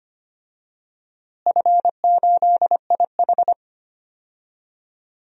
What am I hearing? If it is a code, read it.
Morse code, heard as a keyed tone, about 25 wpm, 700 Hz. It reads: F8IH
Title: DXCC-100 Call Signs - 4 Characters - Repeated 3x - 25wpm